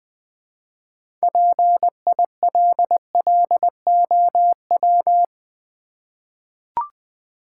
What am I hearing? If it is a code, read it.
Morse code, heard as a keyed tone, about 20 wpm, 700 Hz. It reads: PILLOW
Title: Easy Words with Hints - 20wpm